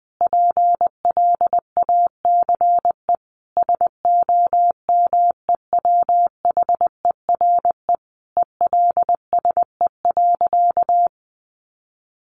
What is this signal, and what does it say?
Morse code, heard as a keyed tone, about 20 wpm, 700 Hz. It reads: PLACE SOMEWHERE ELSE.
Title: Sentences from Top 1400 Words 20wpm